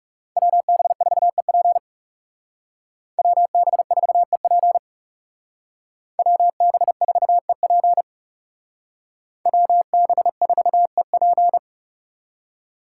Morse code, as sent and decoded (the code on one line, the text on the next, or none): .-- -... ....- . .--. / .-- -... ....- . .--. / .-- -... ....- . .--. / .-- -... ....- . .--.
WB4EP WB4EP WB4EP WB4EP